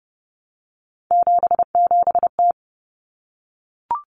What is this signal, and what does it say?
Morse code, heard as a keyed tone, about 30 wpm, 700 Hz. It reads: 77T
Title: Common Cut Numbers - 3-Digit 30wpm